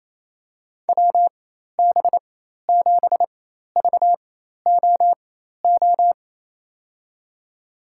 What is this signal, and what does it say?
Morse code, heard as a keyed tone, about 28 wpm, spaced as for 13 wpm, 700 Hz. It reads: WB7VOO
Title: USA Call Signs - 4x Character Spacing - 28wpm